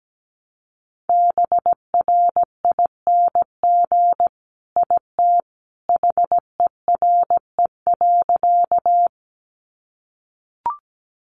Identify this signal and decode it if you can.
Morse code, heard as a keyed tone, about 17 wpm, 700 Hz. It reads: BRING IT HERE.
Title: Sentences from Top 500 Words 17wpm